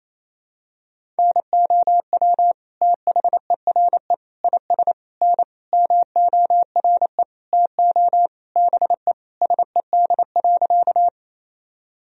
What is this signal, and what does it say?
Morse code, heard as a keyed tone, about 28 wpm, 700 Hz. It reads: NOW THERE IS N MORE TO BE SED.
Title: Sentences from Top 100 Words and Abbreviations 28wpm